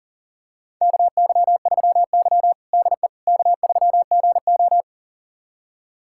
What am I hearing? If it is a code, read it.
Morse code, heard as a keyed tone, about 40 wpm, 700 Hz. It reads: KY3Y DE K3GO
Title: USA Callsign DE Callsign - 4 Characters - 40wpm